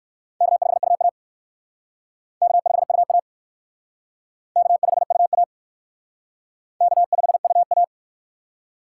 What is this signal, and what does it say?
Morse code, heard as a keyed tone, about 56 wpm, 700 Hz. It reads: K5UA K5UA K5UA K5UA